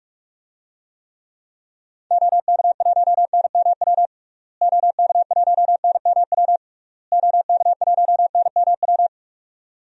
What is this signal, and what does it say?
Morse code, heard as a keyed tone, about 46 wpm, 700 Hz. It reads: OK1NMW OK1NMW OK1NMW
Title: DXCC-100 Call Signs - 6 Characters - Repeated 3x - 45wpm